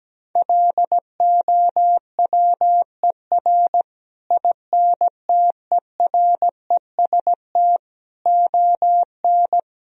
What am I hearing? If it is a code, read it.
Morse code, heard as a keyed tone, about 17 wpm, 700 Hz. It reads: LOWER INTEREST ON
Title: Sentences from Top 1000 Words - Review Part 4 - 17wpm